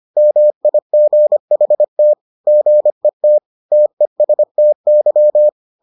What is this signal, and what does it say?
Morse code, heard as a keyed tone, about 25 wpm, 600 Hz. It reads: MIGHT GET TESTY